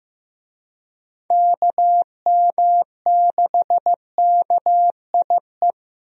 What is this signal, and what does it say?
Morse code, heard as a keyed tone, about 15 wpm, 700 Hz. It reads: KM6KIE
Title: USA Call Signs - 6 Characters - Repeated 3x - 15wpm